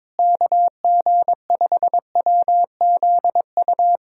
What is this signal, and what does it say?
Morse code, heard as a keyed tone, about 22 wpm, 700 Hz. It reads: KG5WZU
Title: USA Callsign DE Callsign - 6 Characters - 22wpm